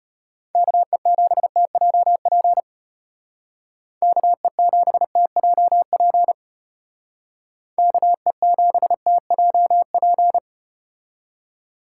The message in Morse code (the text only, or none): KE7TJP KE7TJP KE7TJP